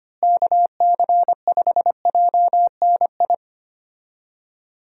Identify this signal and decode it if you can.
Morse code, heard as a keyed tone, about 25 wpm, 700 Hz. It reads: KC5JNI
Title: USA Call Signs - 6 Characters - Repeated 3x - 25wpm